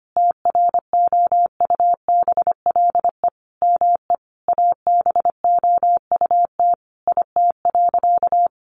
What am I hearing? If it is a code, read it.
Morse code, heard as a keyed tone, about 25 wpm, 700 Hz. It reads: TROUBLE ME ABOUT IT.
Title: Sentences from Top 1200 Words 25wpm